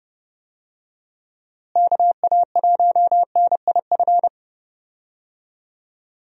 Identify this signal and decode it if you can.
Morse code, heard as a keyed tone, about 30 wpm, 700 Hz. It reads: KA1NIF
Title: USA Call Signs - 6 Characters - Repeated 3x - 30wpm